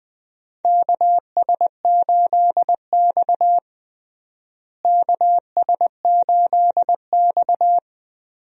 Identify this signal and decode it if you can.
Morse code, heard as a keyed tone, about 20 wpm, 700 Hz. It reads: KS8X KS8X
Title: USA Call Signs - 4 Characters - Repeated 3x - 20wpm